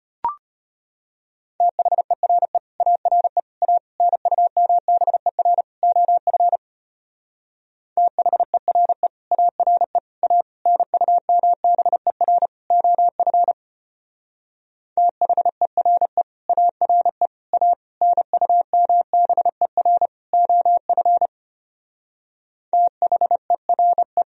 Morse code, heard as a keyed tone, about 30 wpm, 700 Hz. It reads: THERE ARE A NUMBER OF THERE ARE A NUMBER OF THERE ARE A NUMBER OF THERE
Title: Speed-Racing - Sets of 5 Words 25wpm